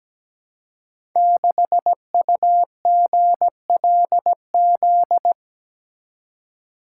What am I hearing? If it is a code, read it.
Morse code, heard as a keyed tone, about 17 wpm, 700 Hz. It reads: 6UGLZ
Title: Code Groups - 5 Letter:Numbers - 17wpm